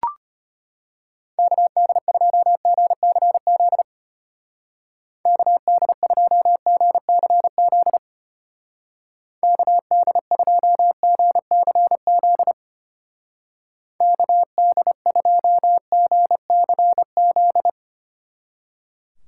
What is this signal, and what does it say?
Morse code, heard as a keyed tone, about 31 wpm, 700 Hz. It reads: KD2GCZ KD2GCZ KD2GCZ KD2GCZ
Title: Speed Racing - USA 2x3 Call Signs 25wpm